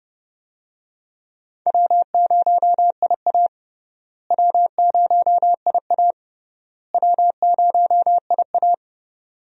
Morse code, heard as a keyed tone, about 30 wpm, 700 Hz. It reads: W0IA W0IA W0IA
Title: USA Call Signs - 4 Characters - Repeated 3x - 30wpm